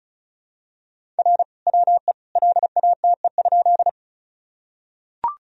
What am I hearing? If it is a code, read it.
Morse code, heard as a keyed tone, about 34 wpm, 700 Hz. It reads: R WE LATE?
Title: Sentences from Top 300 Words and Abbreviations 35wpm